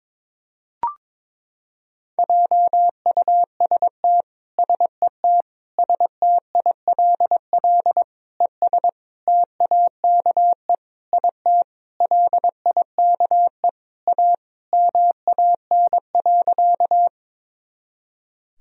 Morse code, heard as a keyed tone, about 22 wpm, 700 Hz. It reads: JUST SET STILL ES TAKE IT LIKE A MAN.
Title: Sentences from Top 300 Words and Abbreviations 22wpm